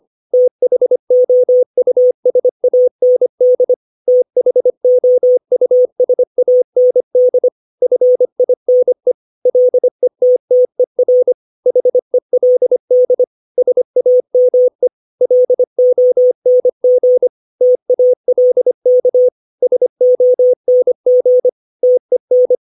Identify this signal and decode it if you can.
Morse code, heard as a keyed tone, about 25 wpm, 500 Hz. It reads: THOUSAND THOUSAND FINE LETTER HELD SAME LONG TALK SONG TEN
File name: 632 words / 632 Words 25WPM